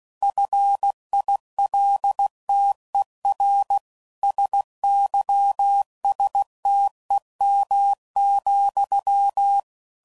Morse code, heard as a keyed tone, about 16 wpm, 800 Hz. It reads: FILTER SYSTEM,
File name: CW Code Groups @15WPM / Technical Subjects / PwrSupply